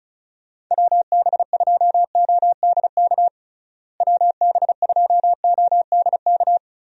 Morse code, heard as a keyed tone, about 36 wpm, 700 Hz. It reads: WB2ODK WB2ODK
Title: USA Call Signs - 6 Characters - Repeated 2x - 35wpm